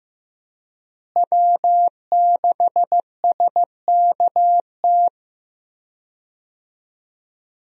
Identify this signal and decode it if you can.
Morse code, heard as a keyed tone, about 15 wpm, 700 Hz. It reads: W6SKT
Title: USA Call Signs - 5 Characters - Repeated 2x - 15wpm